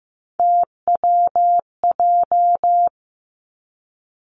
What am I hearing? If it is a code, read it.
Morse code, heard as a keyed tone, about 15 wpm, 700 Hz. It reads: TWJ
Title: Three Challenging Letter-Number Combinations 15wpm